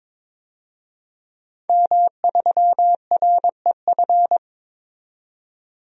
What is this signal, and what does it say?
Morse code, heard as a keyed tone, about 21 wpm, 700 Hz. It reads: M3REF
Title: DXCC-100 Call Signs - 5 Characters - 22wpm